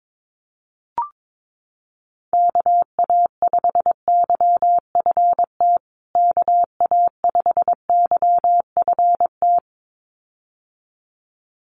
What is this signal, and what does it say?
Morse code, heard as a keyed tone, about 22 wpm, 700 Hz. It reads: KA5YFT KA5YFT
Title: USA Call Signs Repeated 2x 22wpm